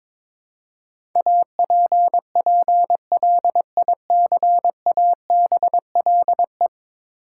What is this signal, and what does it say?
Morse code, heard as a keyed tone, about 22 wpm, 700 Hz. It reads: APPLICABLE
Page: morse-code-ninja-practice-22wpm